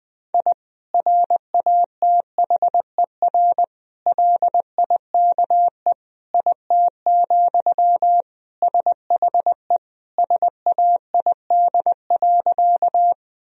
Morse code, heard as a keyed tone, about 20 wpm, 700 Hz. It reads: I RATHER LIKE IT, SHE SAID.
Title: Sentences from Top 1000 Words - Review Part 4 - 20wpm